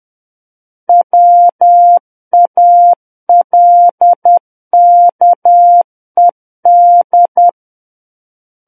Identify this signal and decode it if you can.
Morse code, heard as a keyed tone, about 10 wpm, 700 Hz. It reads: WALKED
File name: Book One / 01-Chapter 1 - 10 WPM